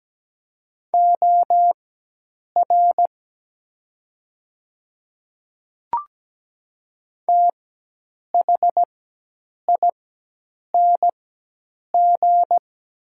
Morse code, heard as a keyed tone, about 17 wpm, spaced as for 8 wpm, 700 Hz. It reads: OR THING
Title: 4x Character Spacing - Top 100 Words - 17wpm